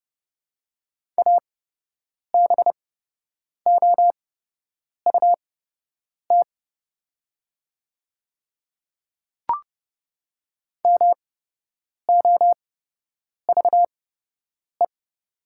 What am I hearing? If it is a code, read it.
Morse code, heard as a keyed tone, about 31 wpm, spaced as for 8 wpm, 700 Hz. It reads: ABOUT MOVE
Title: 8x Character Spacing - Top 300 Words - 30wpm